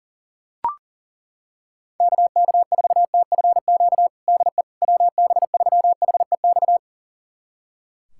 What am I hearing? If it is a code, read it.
Morse code, heard as a keyed tone, about 40 wpm, 700 Hz. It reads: KK4TFQ DE WB3HEX
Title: USA Callsign DE Callsign - 6 Characters - 40wpm